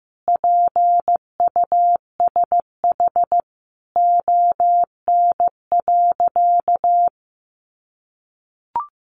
Morse code, heard as a keyed tone, about 15 wpm, 700 Hz. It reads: PUSH ON.
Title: Sentences from Top 1300 Words 15wpm